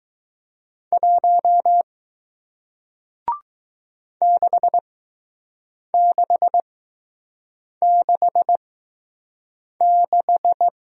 Morse code, heard as a keyed tone, about 20 wpm, spaced as for 6 wpm, 700 Hz. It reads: 1 6666